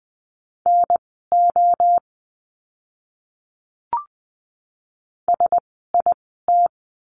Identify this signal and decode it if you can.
Morse code, heard as a keyed tone, about 20 wpm, spaced as for 15 wpm, 700 Hz. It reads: NO SIT
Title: 054 Lesson 10f - Letter r - Words 20wpm